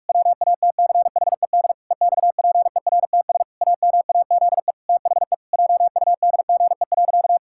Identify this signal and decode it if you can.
Morse code, heard as a keyed tone, about 41 wpm, 700 Hz. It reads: WATCHED EXPERTS AMAZE THE JUDGE.
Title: Pangram Challenge v2 - 45wpm